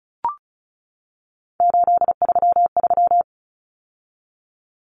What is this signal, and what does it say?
Morse code, heard as a keyed tone, about 34 wpm, 700 Hz. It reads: 833
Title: Numbers - 3-Digit 35wpm